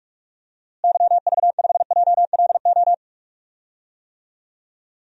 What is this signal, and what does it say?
Morse code, heard as a keyed tone, about 46 wpm, 700 Hz. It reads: YV5JLO
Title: DXCC-100 Call Signs - 6 Characters - Repeated 3x - 45wpm